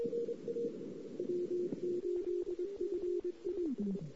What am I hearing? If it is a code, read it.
Morse code, heard as a keyed tone, about 22 wpm, 360 Hz. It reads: W8FI
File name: CW / 2008-ss-cw-k5tr-op-n5rz -06:00